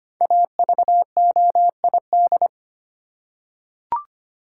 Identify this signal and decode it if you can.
Morse code, heard as a keyed tone, about 25 wpm, 700 Hz. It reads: AVOID
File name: 5 Letter Words - First 3 Letters Given - 25wpm